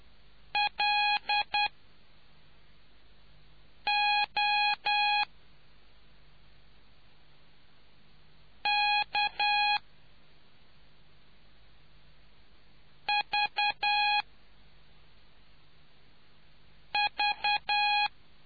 Morse code, heard as a keyed tone, about 10 wpm, spaced as for 3 wpm, 830 Hz. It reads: LOKVV